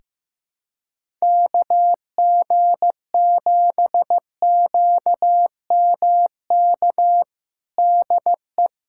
Morse code, heard as a keyed tone, about 15 wpm, 700 Hz. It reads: KG7QMK DE
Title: USA Callsign DE Callsign - 6 Characters - 15wpm